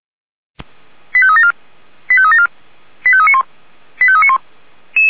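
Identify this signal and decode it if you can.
Five-tone selective call (ZVEI), heard as quick runs of five tones, tones of about 70 ms. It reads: 85374 85374 85391 85391
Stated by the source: ZVEI five-tone selective call 89155, received on SDR.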